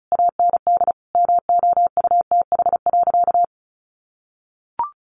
Morse code, heard as a keyed tone, about 34 wpm, 700 Hz. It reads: AND MOUTH.
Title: Easy Phrases 35wpm